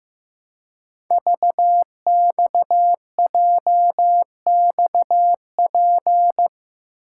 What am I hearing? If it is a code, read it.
Morse code, heard as a keyed tone, about 15 wpm, 700 Hz. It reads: VXJXP
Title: Code Groups - 5 Letter:Numbers - 15wpm